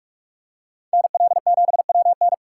Morse code, heard as a keyed tone, about 41 wpm, 700 Hz. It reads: NL7WN
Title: USA Call Signs - 5 Characters - 3x Character Spacing - 45wpm